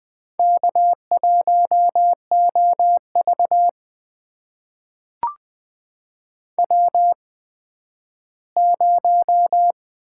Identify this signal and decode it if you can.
Morse code, heard as a keyed tone, about 20 wpm, spaced as for 12 wpm, 700 Hz. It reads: K1OV W0
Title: USA Call Signs - 4 Characters - 8x Character Spacing - 20wpm